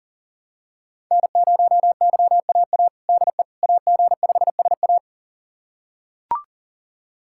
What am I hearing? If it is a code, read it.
Morse code, heard as a keyed tone, about 40 wpm, 700 Hz. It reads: N0YAA DE AG5SA
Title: USA Callsign DE Callsign 40wpm